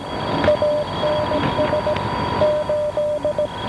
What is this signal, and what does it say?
Morse code, heard as a keyed tone, about 17 wpm, 600 Hz, with noise. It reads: ANS 8
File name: OST5 12781.5 1131